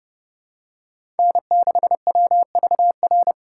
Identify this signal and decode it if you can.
Morse code, heard as a keyed tone, about 31 wpm, 700 Hz. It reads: N6WVR